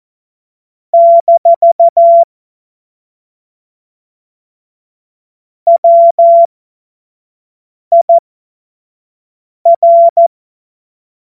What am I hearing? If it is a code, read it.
Morse code, heard as a keyed tone, about 14 wpm, spaced as for 5 wpm, 680 Hz. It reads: - WIR
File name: Morsekurs / Witze / Witz 05 Dezember 2023 25 F70